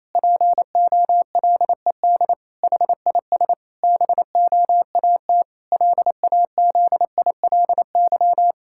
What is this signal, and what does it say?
Morse code, heard as a keyed tone, about 28 wpm, 700 Hz. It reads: POLED HIS BOAT LAZILY
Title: Pangram Challenge v2 - 28wpm